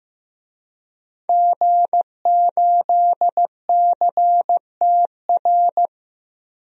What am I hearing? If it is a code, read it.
Morse code, heard as a keyed tone, about 15 wpm, 700 Hz. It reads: G8CTR
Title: DXCC-100 Call Signs - 5 Characters - 15wpm